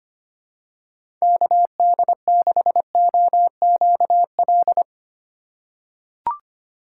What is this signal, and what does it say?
Morse code, heard as a keyed tone, about 25 wpm, 700 Hz. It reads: KD6OQL
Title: USA Call Signs - 4x Character Spacing - 25wpm